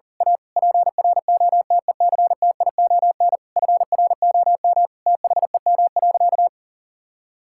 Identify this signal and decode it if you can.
Morse code, heard as a keyed tone, about 40 wpm, 700 Hz. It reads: A PROTECTION FROM THEM.
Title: Sentences from Top 1600 Words 40wpm